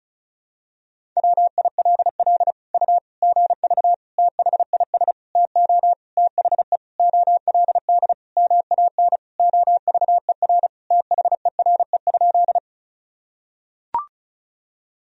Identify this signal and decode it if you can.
Morse code, heard as a keyed tone, about 34 wpm, 700 Hz. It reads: WILL U GV THIS TO THE OLD MAN OVER THERE?
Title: Sentences from Top 200 Words and Abbreviations 35wpm